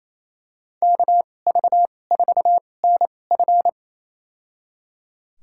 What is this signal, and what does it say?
Morse code, heard as a keyed tone, about 28 wpm, spaced as for 20 wpm, 700 Hz. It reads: KV4NF
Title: USA Call Signs - 5 Characters - 2x Character Spacing - 28wpm